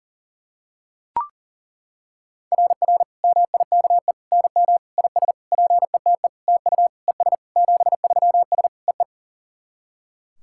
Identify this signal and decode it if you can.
Morse code, heard as a keyed tone, about 37 wpm, 690 Hz. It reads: RR MIKE NM IS PETE TU ES 73S EE
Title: SOTA QSO Practice 40wpm